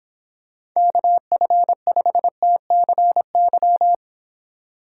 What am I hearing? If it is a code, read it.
Morse code, heard as a keyed tone, about 26 wpm, 700 Hz. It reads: KF5TCY